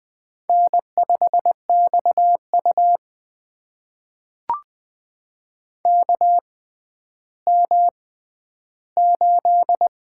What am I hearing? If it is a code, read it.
Morse code, heard as a keyed tone, about 20 wpm, spaced as for 11 wpm, 700 Hz. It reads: N5XU KM8
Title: USA Call Signs - 4 Characters - 6x Character Spacing - 20wpm